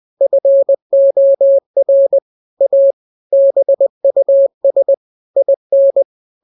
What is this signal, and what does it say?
Morse code, heard as a keyed tone, about 20 wpm, 550 Hz. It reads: FOR A BUS IN
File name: jokes / joke 25 20 0000